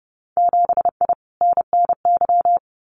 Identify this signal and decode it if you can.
Morse code, heard as a keyed tone, about 30 wpm, 700 Hz. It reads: 7I NNY
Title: Winter Field Day Exchanges Encore 30wpm